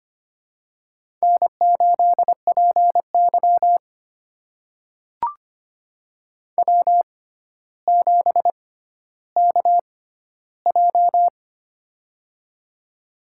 Morse code, heard as a keyed tone, about 25 wpm, spaced as for 13 wpm, 700 Hz. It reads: N8PY W7KJ
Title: USA Call Signs - 4 Characters - 6x Character Spacing - 25wpm